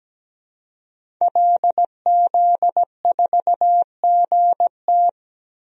Morse code, heard as a keyed tone, about 17 wpm, 700 Hz. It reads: LZ4GT
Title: DXCC-100 Call Signs - 5 Characters - 17wpm